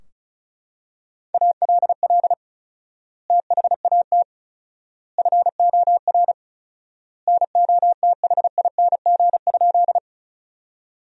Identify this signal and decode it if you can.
Morse code, heard as a keyed tone, about 34 wpm, 700 Hz. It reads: ALL THAT FOR NOTHING?
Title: 4x Word-Spacing - Sentences from Top 500 Words - 35wpm